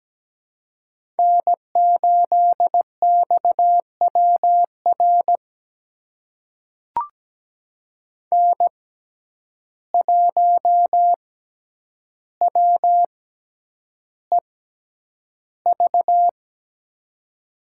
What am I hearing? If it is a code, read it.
Morse code, heard as a keyed tone, about 17 wpm, spaced as for 9 wpm, 700 Hz. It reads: N8XWR N1WEV
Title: USA Call Signs - 5 Characters - 6x Character Spacing - 17wpm